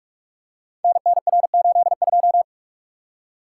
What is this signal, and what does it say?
Morse code, heard as a keyed tone, about 44 wpm, 700 Hz. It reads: NNR82
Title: Code Groups - 5 Letter:Numbers - 45wpm